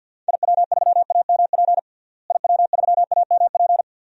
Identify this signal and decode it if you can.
Morse code, heard as a keyed tone, about 48 wpm, 700 Hz. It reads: IW3AMP IW3AMP
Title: DXCC-100 Call Signs - 6 Characters - Repeated 3x - 50wpm